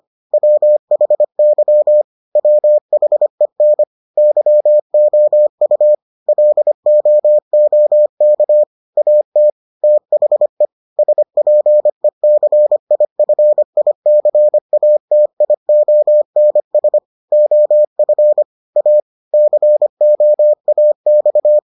Morse code, heard as a keyed tone, about 25 wpm, 600 Hz. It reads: WHY WHEN YOU LOOK AT THE SPECIFICATIONS OF A COAX